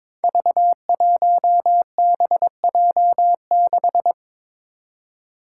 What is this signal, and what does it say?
Morse code, heard as a keyed tone, about 22 wpm, 700 Hz. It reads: V1BJ6
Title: Five Challenging Letter-Number Combinations 22wpm